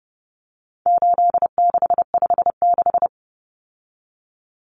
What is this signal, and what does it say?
Morse code, heard as a keyed tone, about 30 wpm, 700 Hz. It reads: 8656
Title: Common Cut Numbers - 4-Digit 30wpm